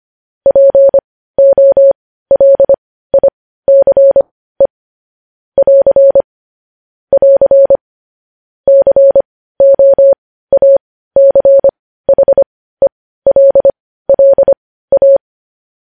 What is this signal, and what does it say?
Morse code, heard as a keyed tone, about 25 wpm, spaced as for 15 wpm, 550 Hz. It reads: POLICE <AR> <AR> COACHELLA
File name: podcast / News headlines-1x-25x15